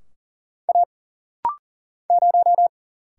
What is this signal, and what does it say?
Morse code, heard as a keyed tone, about 40 wpm, 700 Hz. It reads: A 0
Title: Single Letter-Number Alt - Rapid-Fire 40wpm